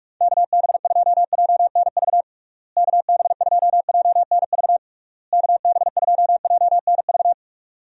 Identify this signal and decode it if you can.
Morse code, heard as a keyed tone, about 41 wpm, 700 Hz. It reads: KB2JNV KB2JNV KB2JNV
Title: USA Call Signs - 6 Characters - Repeated 3x - 45wpm